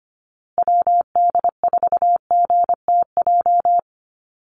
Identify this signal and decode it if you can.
Morse code, heard as a keyed tone, about 25 wpm, 700 Hz. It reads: WD4GTJ
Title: USA Call Signs Repeated 2x 25wpm